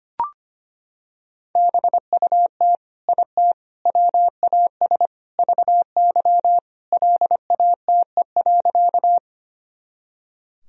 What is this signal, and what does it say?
Morse code, heard as a keyed tone, about 25 wpm, 700 Hz. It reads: BUT IT WAS VY LATE.
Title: Sentences from Top 300 Words and Abbreviations 25wpm